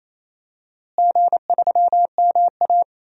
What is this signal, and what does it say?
Morse code, heard as a keyed tone, about 28 wpm, 700 Hz. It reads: G3MA
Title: DXCC-100 Call Signs - 4 Characters - Repeated 2x - 28wpm